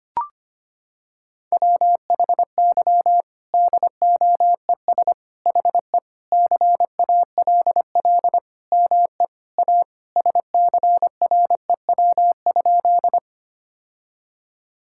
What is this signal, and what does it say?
Morse code, heard as a keyed tone, about 25 wpm, 700 Hz. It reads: WHY DOES HE CALL ME A SCREW?
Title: Sentences from Top 1600 Words 25wpm